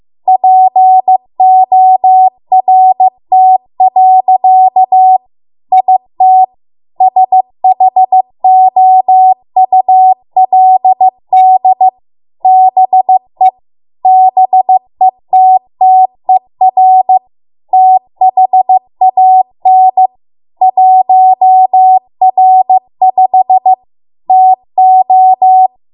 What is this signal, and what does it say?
Morse code, heard as a keyed tone, about 15 wpm, 750 Hz. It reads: PORT. IT SHOULD BE BETTER THAN 1R5 TO